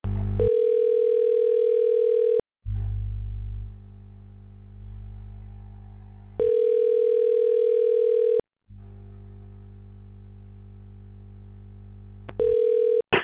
Ringback tone (North America), 440 + 480 Hz.